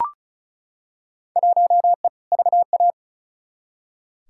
Morse code, heard as a keyed tone, about 36 wpm, 700 Hz. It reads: 1E VA
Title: Field Day Exchanges - 35wpm